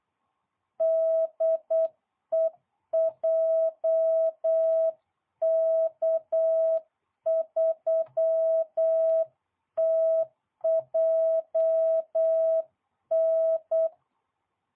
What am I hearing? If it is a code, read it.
Morse code, heard as a keyed tone, about 8 wpm, 660 Hz. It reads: DE JK3TJN